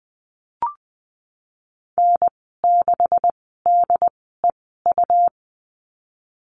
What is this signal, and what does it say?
Morse code, heard as a keyed tone, about 20 wpm, spaced as for 15 wpm, 700 Hz. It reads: N6DEU
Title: 167 Lesson 27g - Number 7 - Call Signs 20wpm